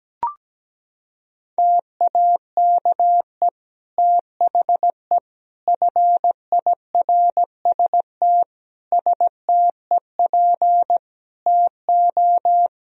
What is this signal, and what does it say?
Morse code, heard as a keyed tone, about 17 wpm, 700 Hz. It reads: TAKE THE FIRST STEP TO
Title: Sentences from Top 1000 Words - Review Part 4 - 17wpm